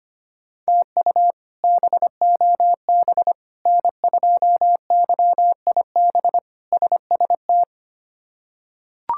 Morse code, heard as a keyed tone, about 25 wpm, 700 Hz. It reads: TU BOB N2YIB SST
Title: K1USN SST QSO Practice - 25wpm